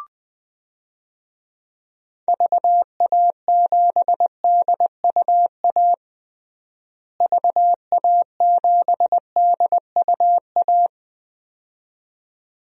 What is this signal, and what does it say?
Morse code, heard as a keyed tone, about 20 wpm, 700 Hz. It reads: VA7DUA VA7DUA
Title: DXCC-100 Call Signs - 6 Characters - Repeated 2x - 20wpm